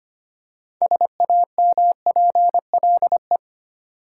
Morse code, heard as a keyed tone, about 25 wpm, 700 Hz. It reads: SAMPLE
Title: Six Letter Words - 25wpm